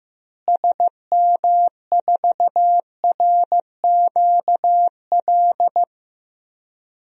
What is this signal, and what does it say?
Morse code, heard as a keyed tone, about 15 wpm, 700 Hz. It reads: SM4RQL
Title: DXCC-100 Call Signs - 6 Characters - Repeated 2x - 15wpm